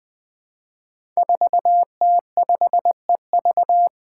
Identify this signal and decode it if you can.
Morse code, heard as a keyed tone, about 20 wpm, 700 Hz. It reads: 4T5EV
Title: Five Challenging Letter-Number Combinations 20wpm